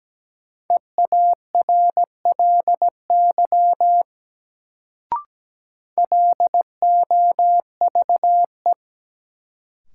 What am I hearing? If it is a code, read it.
Morse code, heard as a keyed tone, about 17 wpm, 700 Hz. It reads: EARLY LOVE
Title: Sentences from Top 400 Words - New Word Drill - 17wpm